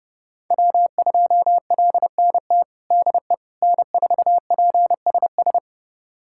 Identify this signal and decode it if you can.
Morse code, heard as a keyed tone, about 31 wpm, 700 Hz. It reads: W2LNT DE N4PSS
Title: USA Callsign DE Callsign - 5 Characters - 30wpm